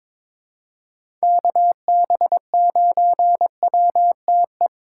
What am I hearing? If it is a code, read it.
Morse code, heard as a keyed tone, about 22 wpm, 700 Hz. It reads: KB9WTE
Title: USA Call Signs - 6 Characters - 2x Character Spacing - 22wpm